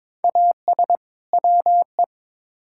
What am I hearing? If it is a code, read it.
Morse code, heard as a keyed tone, about 22 wpm, 700 Hz. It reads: AS WE
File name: Build A Phrase - 2-Word Phrases - 22wpm